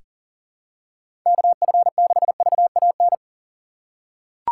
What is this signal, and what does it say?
Morse code, heard as a keyed tone, about 38 wpm, 700 Hz. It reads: KF6VAN
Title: USA Call Signs - 6 Characters - 2x Character Spacing - 40wpm